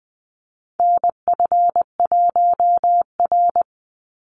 Morse code, heard as a keyed tone, about 20 wpm, 700 Hz. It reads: NF1R
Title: NA SOTA Chaser Callsigns 20wpm